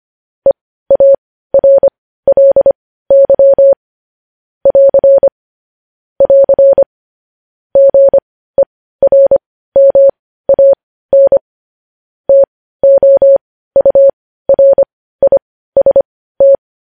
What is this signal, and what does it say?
Morse code, heard as a keyed tone, about 25 wpm, spaced as for 15 wpm, 550 Hz. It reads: EARLY <AR> <AR> GERMAN TOURIST
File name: podcast / News headlines-1x-25x15